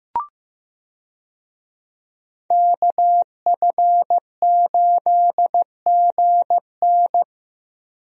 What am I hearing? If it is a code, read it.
Morse code, heard as a keyed tone, about 16 wpm, 700 Hz. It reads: KF8GN
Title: USA Call Signs - 5 Characters - Repeated 2x - 15wpm